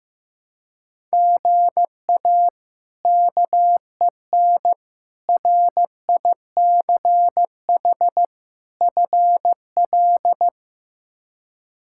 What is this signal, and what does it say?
Morse code, heard as a keyed tone, about 15 wpm, 700 Hz. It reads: GA KEN RICH FL
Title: K1USN SST QSO Practice - 15wpm